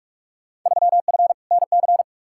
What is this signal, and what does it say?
Morse code, heard as a keyed tone, about 42 wpm, 700 Hz. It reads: 3F NC
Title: Field Day Exchanges Encore - 45wpm